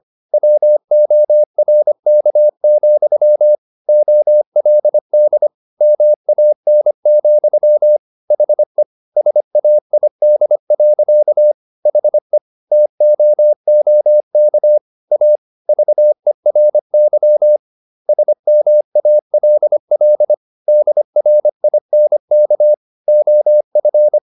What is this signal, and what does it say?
Morse code, heard as a keyed tone, about 25 wpm, 600 Hz. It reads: WORK, OLD MAN, HE SAID. HE TOOK A VERY SMALL DRINK OF